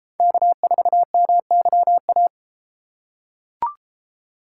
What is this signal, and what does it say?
Morse code, heard as a keyed tone, about 33 wpm, 700 Hz. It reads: K4MYA